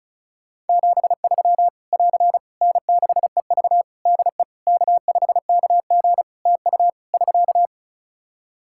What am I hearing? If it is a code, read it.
Morse code, heard as a keyed tone, about 34 wpm, 700 Hz. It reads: 73 <AR> N6EV DE K5KG TU <SK>
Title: K1USN SST QSO Practice 35wpm